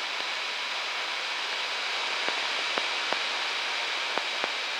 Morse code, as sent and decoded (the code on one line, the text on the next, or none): none